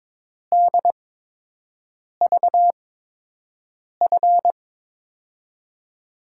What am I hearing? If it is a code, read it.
Morse code, heard as a keyed tone, about 22 wpm, spaced as for 6 wpm, 700 Hz. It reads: DVF
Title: USA Call Signs - 6 Characters - 8x Character Spacing - 22wpm